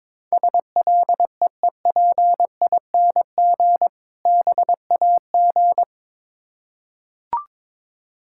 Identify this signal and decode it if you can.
Morse code, heard as a keyed tone, about 22 wpm, 700 Hz. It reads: SLEEPING BAG
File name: Difficult Words with Hints - 22wpm